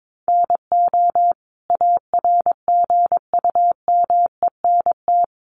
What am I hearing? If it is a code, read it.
Morse code, heard as a keyed tone, about 22 wpm, 700 Hz. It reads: NO ARGUMENT
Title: Sentences from Top 1400 Words 22wpm